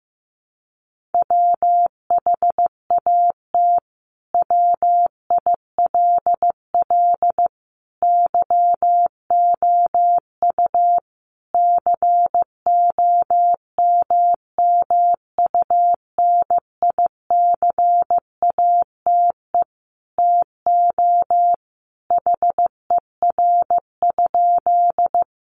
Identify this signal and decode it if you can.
Morse code, heard as a keyed tone, about 15 wpm, 700 Hz. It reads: WHAT WILL YOU COMMUNICATE TO HER?
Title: Sentences from Top 1300 Words 15wpm